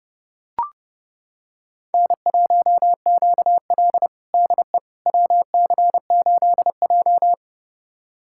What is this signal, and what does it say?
Morse code, heard as a keyed tone, about 30 wpm, 700 Hz. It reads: N1QL DE WC8J
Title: USA Callsign DE Callsign - 4 Characters - 30wpm